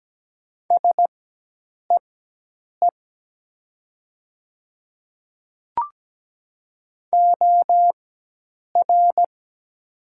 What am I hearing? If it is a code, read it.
Morse code, heard as a keyed tone, about 17 wpm, spaced as for 8 wpm, 700 Hz. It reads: SEE OR
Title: 4x Character Spacing - Top 100 Words - 17wpm